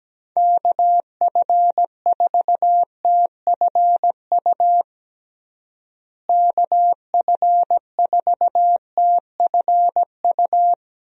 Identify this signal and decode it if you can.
Morse code, heard as a keyed tone, about 17 wpm, 700 Hz. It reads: KF4TFU KF4TFU
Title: USA Call Signs - 6 Characters - Repeated 3x - 17wpm